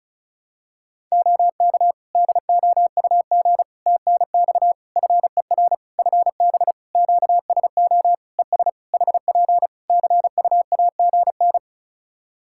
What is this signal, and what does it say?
Morse code, heard as a keyed tone, about 36 wpm, 700 Hz. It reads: OK DOUG TNX FER FB QSO ES HP CUAGN
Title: Practice QSO - Exchange 4 - Ending - 35wpm